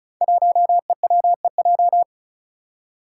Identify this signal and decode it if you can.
Morse code, heard as a keyed tone, about 34 wpm, 700 Hz. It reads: 1EWEJ
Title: Five Challenging Letter-Number Combinations 35wpm